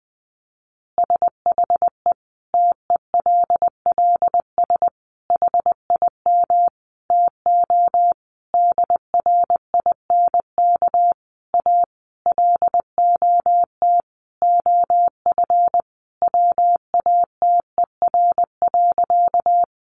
Morse code, heard as a keyed tone, about 20 wpm, 700 Hz. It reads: SHE TELLS HIM TO DRINK A LOT OF WATER.
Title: Additional Easy Phrases 20wpm